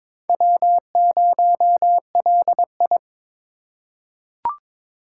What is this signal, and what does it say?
Morse code, heard as a keyed tone, about 23 wpm, 700 Hz. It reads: W0LI
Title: USA Call Signs - 4 Characters - 2x Character Spacing - 22wpm